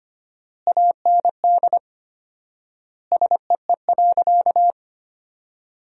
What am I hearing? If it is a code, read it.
Morse code, heard as a keyed tone, about 25 wpm, 700 Hz. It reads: AND SEE.
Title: 4x Word-Spacing - Sentences from Top 100 Words - 25wpm